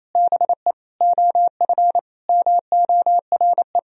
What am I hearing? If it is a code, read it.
Morse code, heard as a keyed tone, about 28 wpm, 700 Hz. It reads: BE OF MORE
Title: Sentences from Top 400 Words and Abbreviations 28wpm